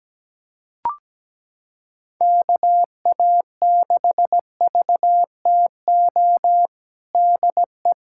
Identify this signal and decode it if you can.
Morse code, heard as a keyed tone, about 17 wpm, 700 Hz. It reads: KA6VTO DE
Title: USA Callsign DE Callsign 17wpm